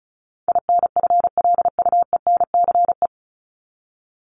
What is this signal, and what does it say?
Morse code, heard as a keyed tone, about 34 wpm, 700 Hz. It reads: INFLUENCE
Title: Top 1001-1250 Words 35wpm